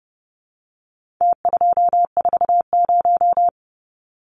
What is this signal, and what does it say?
Morse code, heard as a keyed tone, about 30 wpm, 700 Hz. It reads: T240
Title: Numbers - Contest Style - 4-Digit 30wpm